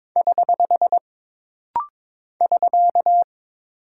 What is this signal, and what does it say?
Morse code, heard as a keyed tone, about 22 wpm, 700 Hz. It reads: <HH> <SK>
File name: practice / 22wpm / Single Punctuation-Prosign - Rapid-Fire - 22wpm